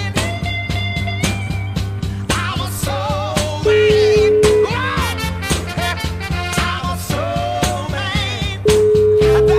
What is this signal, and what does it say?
Ringback tone (Europe, 425 Hz), with noise.